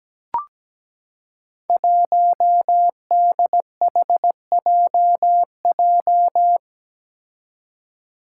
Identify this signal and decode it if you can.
Morse code, heard as a keyed tone, about 17 wpm, 700 Hz. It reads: 1DHJJ
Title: Five Challenging Letter-Number Combinations 17wpm